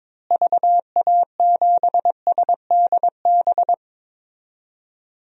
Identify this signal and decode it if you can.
Morse code, heard as a keyed tone, about 22 wpm, 700 Hz. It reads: VA7SDB
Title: DXCC-100 Call Signs - 6 Characters - Repeated 2x - 22wpm